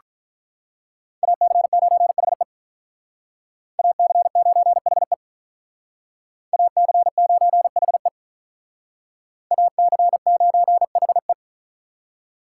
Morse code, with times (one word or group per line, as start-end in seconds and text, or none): 1.2-2.4 AC9HE
3.8-5.1 AC9HE
6.5-8.1 AC9HE
9.5-11.3 AC9HE